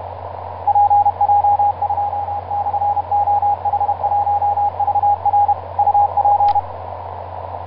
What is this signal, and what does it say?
Morse code, heard as a keyed tone, about 30 wpm, 820 Hz, with radio noise. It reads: PJ2/OH1VR UP
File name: pj2-oh1vr iz2amw etc 12m cw 24mar2010 1955